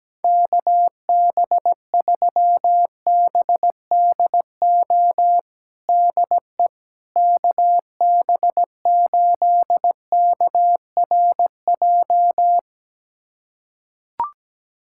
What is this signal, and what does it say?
Morse code, heard as a keyed tone, about 17 wpm, 700 Hz. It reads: KB3BDO DE KB8KRJ
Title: USA Callsign DE Callsign 17wpm